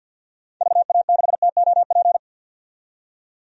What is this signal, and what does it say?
Morse code, heard as a keyed tone, about 47 wpm, 700 Hz. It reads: VA6TOP
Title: DXCC-100 Call Signs - 6 Characters - 50wpm